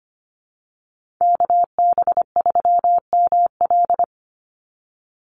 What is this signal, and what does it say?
Morse code, heard as a keyed tone, about 25 wpm, 700 Hz. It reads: KB3ML
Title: Call Signs Repeated 3x - 25wpm